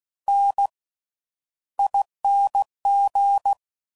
Morse code, heard as a keyed tone, about 16 wpm, spaced as for 11 wpm, 800 Hz. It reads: NING
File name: Slow With Faster Ings